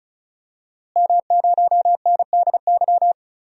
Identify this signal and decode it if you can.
Morse code, heard as a keyed tone, about 36 wpm, 700 Hz. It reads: M0NDY
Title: DXCC-100 Call Signs - 5 Characters - Repeated 3x - 35wpm